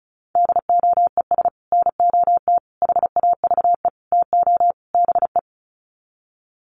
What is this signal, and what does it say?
Morse code, heard as a keyed tone, about 34 wpm, 700 Hz. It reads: DOES NOT HAVE TO BE
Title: Sets of 5 Words 35wpm